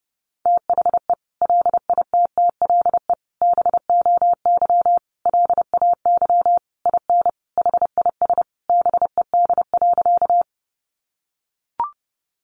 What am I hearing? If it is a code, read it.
Morse code, heard as a keyed tone, about 30 wpm, 700 Hz. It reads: THE LITTLE BOY LAY IN HIS BED.
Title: Sentences from Top 500 Words 30wpm